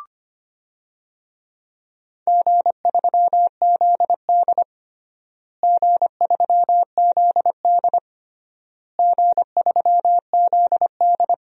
Morse code, heard as a keyed tone, about 25 wpm, 700 Hz. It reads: G3ZD G3ZD G3ZD
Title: DXCC-100 Call Signs - 4 Characters - Repeated 3x - 25wpm